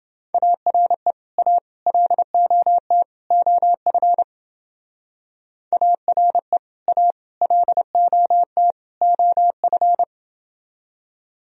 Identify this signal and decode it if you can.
Morse code, heard as a keyed tone, about 29 wpm, 700 Hz. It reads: ARE A LOT OF ARE A LOT OF